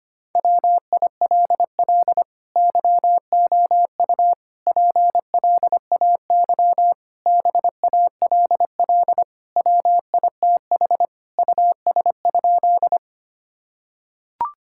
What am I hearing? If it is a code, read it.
Morse code, heard as a keyed tone, about 25 wpm, 700 Hz. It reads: WILL YOU PLAY BALL WITH US?
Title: Sentences from Top 1000 Words - Review Part 6 - 25wpm